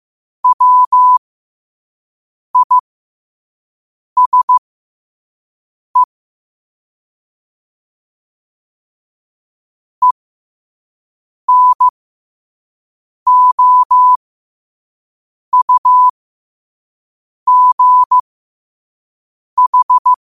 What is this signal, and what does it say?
Morse code, heard as a keyed tone, about 15 wpm, spaced as for 5 wpm, 1000 Hz. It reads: WISE ENOUGH
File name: qotdfiles / 5wpm